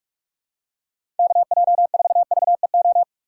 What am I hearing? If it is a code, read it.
Morse code, heard as a keyed tone, about 46 wpm, 700 Hz. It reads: KJ4VEO
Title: USA Call Signs - 3x Character Spacing - 45wpm